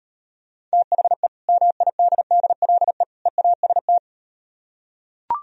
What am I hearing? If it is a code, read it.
Morse code, heard as a keyed tone, about 38 wpm, 700 Hz. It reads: THE MIDDLE EAST